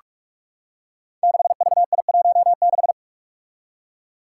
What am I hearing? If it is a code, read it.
Morse code, heard as a keyed tone, about 44 wpm, 700 Hz. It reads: 6VI16
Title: Five Challenging Letter-Number Combinations 45wpm